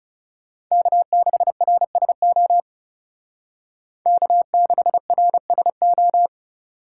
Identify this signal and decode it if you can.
Morse code, heard as a keyed tone, about 36 wpm, 700 Hz. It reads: K6RSO K6RSO
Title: Speed Racing - USA 1x3 Call Signs 30wpm